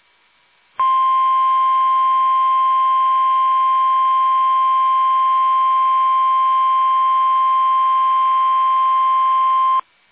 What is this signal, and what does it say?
Emergency Alert System (SAME) tones (the 1050 Hz attention tone). It is heard from 0.8 to 9.8 s.
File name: eas / KUSO / KUSO